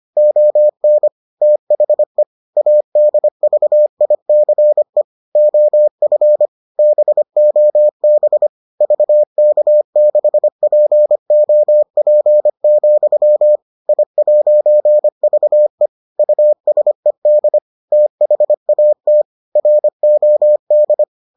Morse code, heard as a keyed tone, about 25 wpm, 600 Hz. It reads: ON THE ADVICE OF BOB VK6POP, I'VE USED THAT ROD